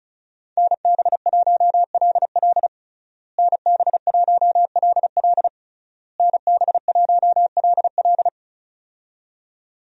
Morse code, heard as a keyed tone, about 34 wpm, 700 Hz. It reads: NB1LL NB1LL NB1LL
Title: USA Call Signs - 5 Characters - Repeated 3x - 35wpm